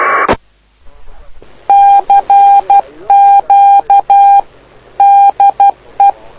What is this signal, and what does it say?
Morse code, heard as a keyed tone, about 12 wpm, 800 Hz. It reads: CQ DE
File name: lu4aao / lu7aa 09dic2011 exp planeador vuelo1